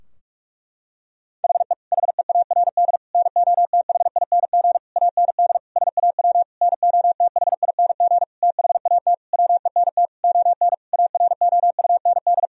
Morse code, heard as a keyed tone, about 41 wpm, 700 Hz. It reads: HE HEARD NOTHING AND SAW NOTHING THAT WENT ON AROUND
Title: Sentences from Top 1000 Words - Review Part 3 - 45wpm